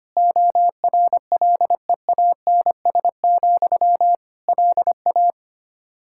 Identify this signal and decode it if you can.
Morse code, heard as a keyed tone, about 25 wpm, 700 Hz. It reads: ORLEANS, LA
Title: K1USN SST QSO Practice 25wpm